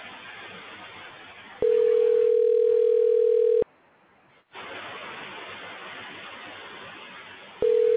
Ringback tone (North America), 440 + 480 Hz.
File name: recordings / RE8b6f7377edf3bdb59a227e a6f